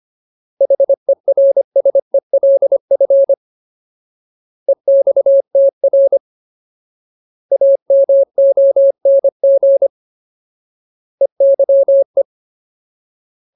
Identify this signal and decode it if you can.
Morse code, heard as a keyed tone, about 25 wpm, 550 Hz. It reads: HERSELF EXTR AMONG EYE
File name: Ultimate Word List / 25WPM Ultimate